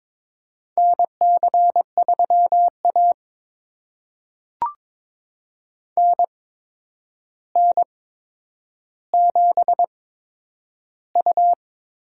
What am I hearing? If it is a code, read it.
Morse code, heard as a keyed tone, about 22 wpm, spaced as for 9 wpm, 700 Hz. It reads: NC3A NN7U
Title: USA Call Signs - 4 Characters - 8x Character Spacing - 22wpm